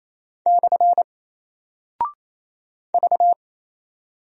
Morse code, heard as a keyed tone, about 27 wpm, 700 Hz. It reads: / V
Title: Single Character (Letter:Number:Punctuation:Prosign) Alt 28wpm